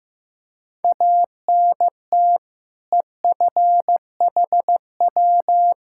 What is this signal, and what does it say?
Morse code, heard as a keyed tone, about 15 wpm, 700 Hz. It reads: ANT EFHW
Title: K1USN SST QSO Practice 15wpm